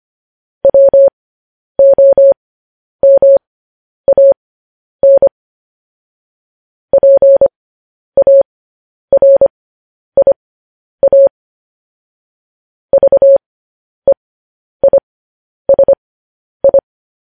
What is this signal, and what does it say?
Morse code, heard as a keyed tone, about 25 wpm, spaced as for 10 wpm, 550 Hz. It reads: WOMAN PARIA VEISI
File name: podcast / News headlines-1x-25x10